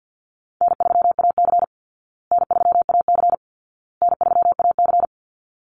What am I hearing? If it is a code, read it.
Morse code, heard as a keyed tone, about 50 wpm, 700 Hz. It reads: N3AC N3AC N3AC
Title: Call Signs Repeated 2x - 50wpm